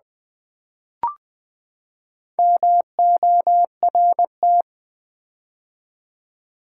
Morse code, heard as a keyed tone, about 20 wpm, 700 Hz. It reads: MORT
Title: Build A Word - 4 Letters At A Time - 8-Letter Words - 20wpm